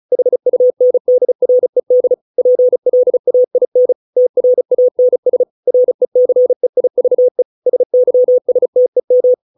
Morse code, heard as a keyed tone, about 34 wpm, 500 Hz. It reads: HUNDRED PLAIN TRANS RECEIVE SYSTEM